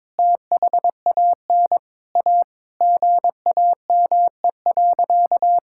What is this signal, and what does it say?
Morse code, heard as a keyed tone, about 22 wpm, 700 Hz. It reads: THAN A GAME.
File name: practice / 22wpm / Sentences with 3-Word Phrases - 22wpm